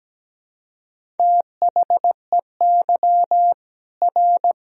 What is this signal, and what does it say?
Morse code, heard as a keyed tone, about 17 wpm, 700 Hz. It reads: THEY R
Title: Sentences from Top 200 Words and Abbreviations 17wpm